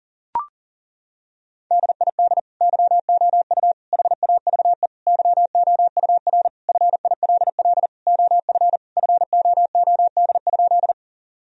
Morse code, heard as a keyed tone, about 40 wpm, 700 Hz. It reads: DID YOU HAVE YOUR FILL OF FOOD?
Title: Sentences from Top 1000 Words - Review Part 3 - 40wpm